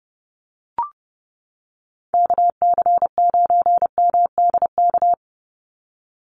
Morse code, heard as a keyed tone, about 30 wpm, 700 Hz. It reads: KC9MDK
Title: USA Call Signs 30wpm